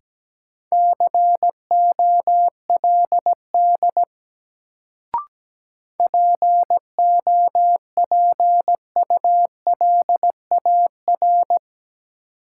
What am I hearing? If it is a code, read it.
Morse code, heard as a keyed tone, about 17 wpm, 700 Hz. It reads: COLD POPULAR
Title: Sets of 5 Words - New Word Drill - 17wpm